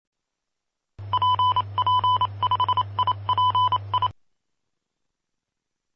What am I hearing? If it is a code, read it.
Morse code, heard as a keyed tone, about 27 wpm, 1000 Hz. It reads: PP5IPI
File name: pp5bar / aulas de telegrafia / Aula-03